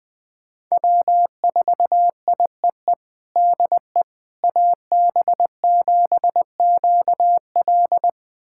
Morse code, heard as a keyed tone, about 20 wpm, 700 Hz. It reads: W4IEE DE AB7QL
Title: USA Callsign DE Callsign - 5 Characters - 20wpm